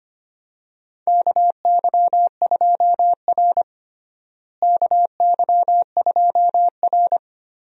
Morse code, heard as a keyed tone, about 25 wpm, 700 Hz. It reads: KY2R KY2R
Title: USA Call Signs - 4 Characters - Repeated 3x - 25wpm